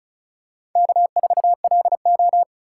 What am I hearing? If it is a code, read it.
Morse code, heard as a keyed tone, about 35 wpm, 700 Hz. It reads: K4LO
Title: USA Call Signs - 4 Characters - 4x Character Spacing - 35wpm